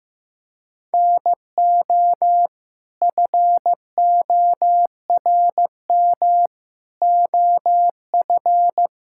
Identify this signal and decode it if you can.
Morse code, heard as a keyed tone, about 15 wpm, 700 Hz. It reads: NO FORM OF
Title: Sentences from Top 1000 Words - Review Part 6 - 15wpm